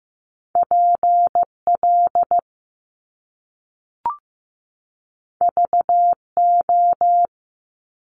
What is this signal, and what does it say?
Morse code, heard as a keyed tone, about 15 wpm, 700 Hz. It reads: PL VO
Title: Top Two Letter Combinations in Words 15wpm